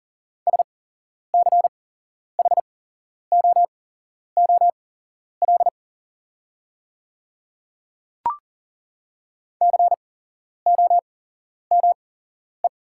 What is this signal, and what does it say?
Morse code, heard as a keyed tone, about 37 wpm, spaced as for 11 wpm, 700 Hz. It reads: SCHOOL COME
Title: 8x Character Spacing - Top 500 Words - 40wpm